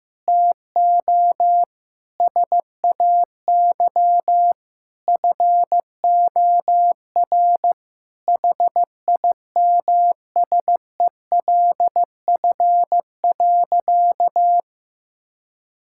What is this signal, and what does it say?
Morse code, heard as a keyed tone, about 16 wpm, 700 Hz. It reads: TO SAY FOR HIMSELF.
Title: Sentences from Top 1600 Words 15wpm